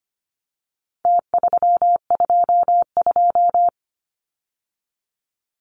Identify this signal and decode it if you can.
Morse code, heard as a keyed tone, about 25 wpm, 700 Hz. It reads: T322
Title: Numbers - Contest Style - 4-Digit 25wpm